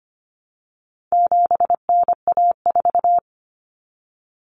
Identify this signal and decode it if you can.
Morse code, heard as a keyed tone, about 25 wpm, 700 Hz. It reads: 7NA4
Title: Common Cut Numbers - 4-Digit 25wpm